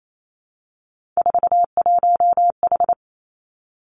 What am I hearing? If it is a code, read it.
Morse code, heard as a keyed tone, about 28 wpm, 700 Hz. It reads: 41H
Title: Three Challenging Letter-Number Combinations 28wpm